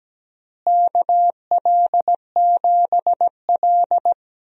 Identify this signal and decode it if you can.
Morse code, heard as a keyed tone, about 17 wpm, 700 Hz. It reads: KL7L
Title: DXCC-100 Call Signs - 4 Characters - Repeated 3x - 17wpm